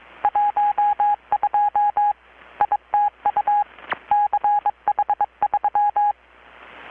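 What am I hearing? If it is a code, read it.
Morse code, heard as a keyed tone, about 22 wpm, 830 Hz. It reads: 12 ITU CH3
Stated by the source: Morse from KPH's Night of Nights XX maritime radiotelegraphy event, received in Chiapas, Mexico about 3,800 km away on a Kenwood R-600 with a grounded 10 m long-wire antenna.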